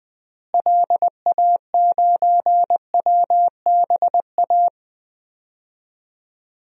Morse code, heard as a keyed tone, about 20 wpm, 700 Hz. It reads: LA9WBA
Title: DXCC-100 Call Signs - 6 Characters - Repeated 3x - 20wpm